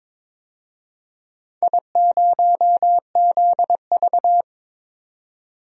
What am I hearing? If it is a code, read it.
Morse code, heard as a keyed tone, about 22 wpm, 700 Hz. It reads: I0ZV
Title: DXCC-100 Call Signs - 4 Characters - 22wpm